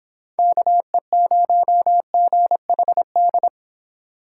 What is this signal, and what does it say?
Morse code, heard as a keyed tone, about 30 wpm, 700 Hz. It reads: KE0GHD